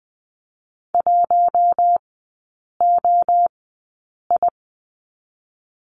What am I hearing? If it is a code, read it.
Morse code, heard as a keyed tone, about 20 wpm, spaced as for 8 wpm, 700 Hz. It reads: 1OI
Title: 067 Lesson 12e - Letter d - Three Characters 20wpm